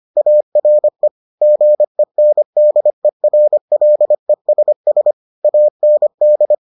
Morse code, heard as a keyed tone, about 25 wpm, 600 Hz. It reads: ARE GENDERLESS AND